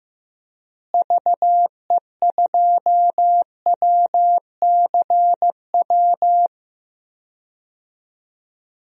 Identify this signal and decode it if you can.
Morse code, heard as a keyed tone, about 16 wpm, 700 Hz. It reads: VE2WCW
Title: DXCC-100 Call Signs - 6 Characters - Repeated 3x - 15wpm